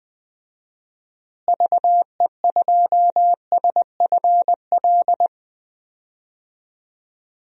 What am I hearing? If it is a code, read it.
Morse code, heard as a keyed tone, about 20 wpm, 700 Hz. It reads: VE2SFL
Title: DXCC-100 Call Signs - 6 Characters - Repeated 3x - 20wpm